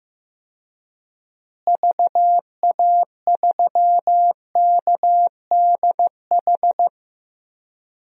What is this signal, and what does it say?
Morse code, heard as a keyed tone, about 15 wpm, 700 Hz. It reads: VA3KDH
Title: DXCC-100 Call Signs - 6 Characters - Repeated 2x - 15wpm